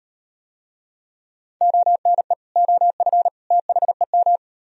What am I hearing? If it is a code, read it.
Morse code, heard as a keyed tone, about 38 wpm, 700 Hz. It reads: ONE OF THEM